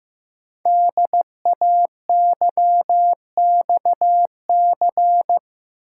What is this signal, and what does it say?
Morse code, heard as a keyed tone, about 15 wpm, 700 Hz. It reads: DAYXC
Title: Code Groups - 5 Letters - 15wpm